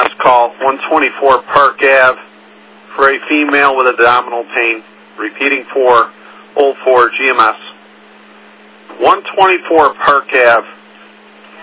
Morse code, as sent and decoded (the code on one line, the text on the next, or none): none